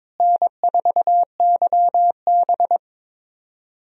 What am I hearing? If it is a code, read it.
Morse code, heard as a keyed tone, about 22 wpm, 700 Hz. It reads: N4YB
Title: USA Call Signs - 4 Characters - Repeated 3x - 22wpm